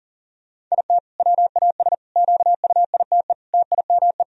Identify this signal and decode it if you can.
Morse code, heard as a keyed tone, about 40 wpm, 700 Hz. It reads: IT WAS QUITE TIME
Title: Sentences from Top 1000 Words - Review Part 1 - 40wpm